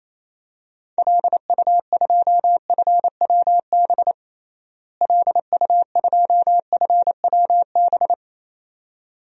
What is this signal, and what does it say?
Morse code, heard as a keyed tone, about 28 wpm, 700 Hz. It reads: LU2FWB LU2FWB
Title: DXCC-100 Call Signs - 6 Characters - Repeated 2x - 28wpm